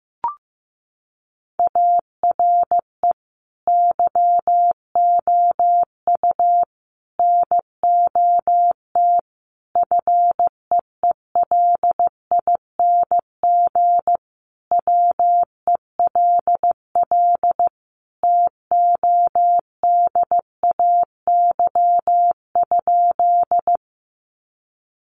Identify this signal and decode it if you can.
Morse code, heard as a keyed tone, about 15 wpm, 700 Hz. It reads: ARE YOU NOT FEELING WELL TODAY?
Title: Sentences from Top 1200 Words 15wpm